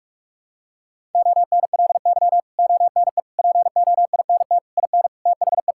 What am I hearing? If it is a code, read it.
Morse code, heard as a keyed tone, about 46 wpm, 700 Hz. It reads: ONLY ONE POINT IN THE
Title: Sentences from Top 1000 Words - Review Part 5 - 45wpm